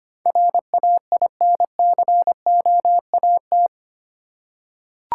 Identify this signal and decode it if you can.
Morse code, heard as a keyed tone, about 25 wpm, 700 Hz. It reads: RAINCOAT
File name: practice / 25wpm / Easy Words with Hints - 25wpm